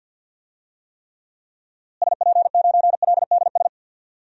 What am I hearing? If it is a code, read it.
Morse code, heard as a keyed tone, about 46 wpm, 700 Hz. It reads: SP9LDS
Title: DXCC-100 Call Signs - 6 Characters - Repeated 2x - 50wpm